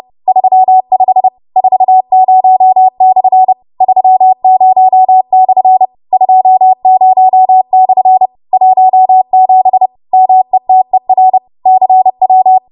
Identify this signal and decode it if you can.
Morse code, heard as a keyed tone, about 30 wpm, 750 Hz. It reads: 35 40/30/20/17 METER CW